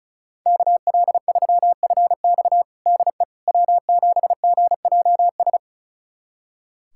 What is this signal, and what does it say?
Morse code, heard as a keyed tone, about 36 wpm, 700 Hz. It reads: KL3FX DE W7GJS
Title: USA Callsign DE Callsign - 5 Characters - 35wpm